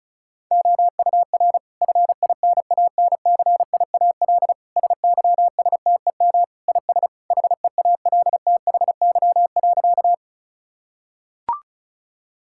Morse code, heard as a keyed tone, about 36 wpm, 700 Hz. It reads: OUR FINANCIAL SYSTEM IS HEALTHY.
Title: Sentences from Top 1000 Words - Review Part 5 - 35wpm